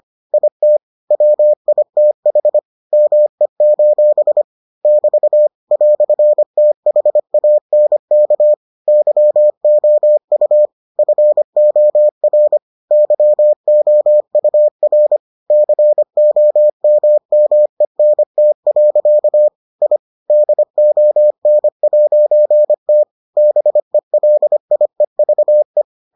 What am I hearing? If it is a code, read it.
Morse code, heard as a keyed tone, about 25 wpm, 600 Hz. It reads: IT WITH ME: <BT> "THANK YOU FOR YOUR COMMENT. I DON'T BELIEVE